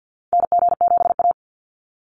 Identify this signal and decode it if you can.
Morse code, heard as a keyed tone, about 50 wpm, 700 Hz. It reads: NG7A
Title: NA SOTA Activator Callsigns 50wpm